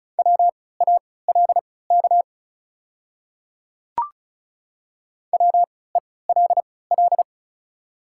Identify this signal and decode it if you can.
Morse code, heard as a keyed tone, about 36 wpm, spaced as for 20 wpm, 700 Hz. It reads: WALK WELL
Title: 3x Character Spacing - Top 300 Words - 35wpm